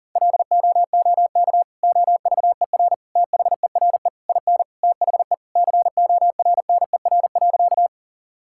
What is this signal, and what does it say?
Morse code, heard as a keyed tone, about 40 wpm, 700 Hz. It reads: LOOK OVER THERE IN THE CORNER.
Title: Sentences from Top 1600 Words 40wpm